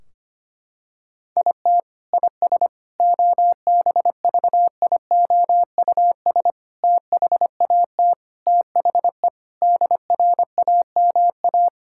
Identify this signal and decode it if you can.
Morse code, heard as a keyed tone, about 25 wpm, 700 Hz. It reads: IT IS OBVIOUS THAT THE DRAMA
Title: Sentences from Top 1600 Words 25wpm